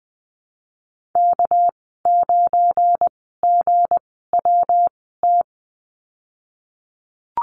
Morse code, heard as a keyed tone, about 20 wpm, 700 Hz. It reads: K 9 G W T
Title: 238 Lesson 38g - Letter q - Call Signs 20wpm